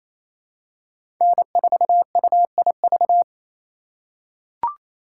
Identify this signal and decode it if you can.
Morse code, heard as a keyed tone, about 28 wpm, 700 Hz. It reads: N4UIV
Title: USA Call Signs - 5 Characters - 4x Character Spacing - 28wpm